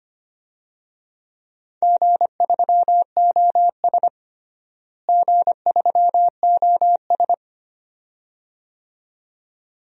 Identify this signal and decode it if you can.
Morse code, heard as a keyed tone, about 25 wpm, 700 Hz. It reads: G3OS G3OS
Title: DXCC-100 Call Signs - 4 Characters - Repeated 2x - 25wpm